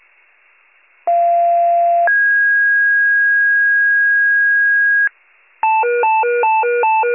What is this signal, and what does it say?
Two-tone page, heard as a one-second tone, then a three-second tone, about 690 then 1650 Hz. It is heard from 1.1 to 5.1 s.